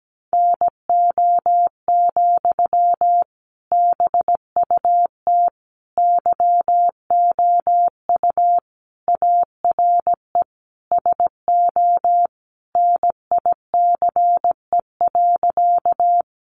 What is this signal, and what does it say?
Morse code, heard as a keyed tone, about 17 wpm, 700 Hz. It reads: NO, BUT YOU ARE SO NICE.
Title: Sentences from Top 1100 Words 17wpm